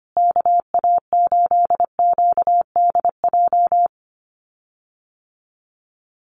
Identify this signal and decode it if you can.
Morse code, heard as a keyed tone, about 25 wpm, 700 Hz. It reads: KA8QDJ
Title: USA Call Signs Repeated 3x 25wpm